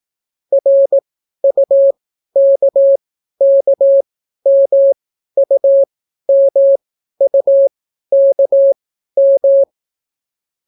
Morse code, heard as a keyed tone, about 18 wpm, spaced as for 12 wpm, 550 Hz. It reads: RUKKMUMUKM